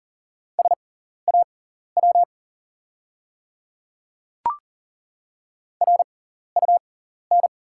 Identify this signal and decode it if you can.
Morse code, heard as a keyed tone, about 39 wpm, spaced as for 14 wpm, 700 Hz. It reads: SAW RUN
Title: 6x Character Spacing - Top 500 Words - 40wpm